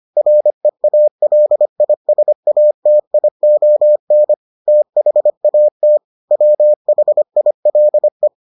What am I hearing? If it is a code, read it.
Morse code, heard as a keyed tone, about 25 wpm, 600 Hz. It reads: REALISATION THAT WHILE